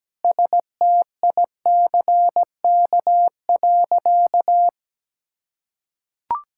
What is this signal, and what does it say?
Morse code, heard as a keyed tone, about 17 wpm, 700 Hz. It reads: STICK.
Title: Sentences from Top 1600 Words 17wpm